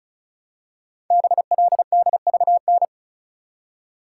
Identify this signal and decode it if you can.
Morse code, heard as a keyed tone, about 36 wpm, 700 Hz. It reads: BLDVN
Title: Code Groups - 5 Letter:Numbers - 35wpm